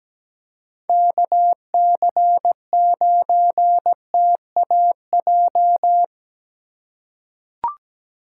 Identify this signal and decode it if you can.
Morse code, heard as a keyed tone, about 17 wpm, 700 Hz. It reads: KC9TAJ
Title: USA Call Signs - 6 Characters - 2x Character Spacing - 17wpm